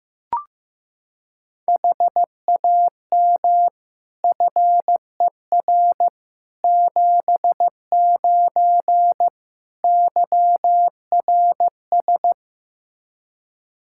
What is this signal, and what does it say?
Morse code, heard as a keyed tone, about 15 wpm, 700 Hz. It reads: HAM FER 79 YRS
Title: Practice QSO - Exchange 3 - More Info - 15wpm